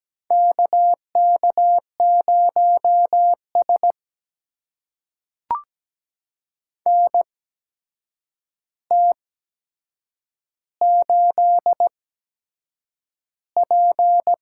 Morse code, heard as a keyed tone, about 17 wpm, spaced as for 7 wpm, 700 Hz. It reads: KK0S NT8P
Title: USA Call Signs - 4 Characters - 8x Character Spacing - 17wpm